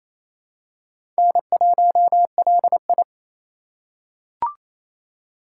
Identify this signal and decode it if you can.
Morse code, heard as a keyed tone, about 28 wpm, 700 Hz. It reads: N1LI
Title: USA Call Signs - 4 Characters - 3x Character Spacing - 28wpm